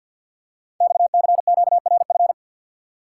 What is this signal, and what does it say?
Morse code, heard as a keyed tone, about 50 wpm, 690 Hz. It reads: XCQRF
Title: Code Groups - 5 Letters - 50wpm